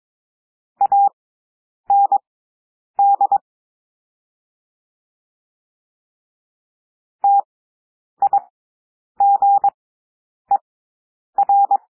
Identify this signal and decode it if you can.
Morse code, heard as a keyed tone, about 22 wpm, spaced as for 9 wpm, 800 Hz. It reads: AND TIGER
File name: cwmp3 / W2 SOUND WORDS 22 wpm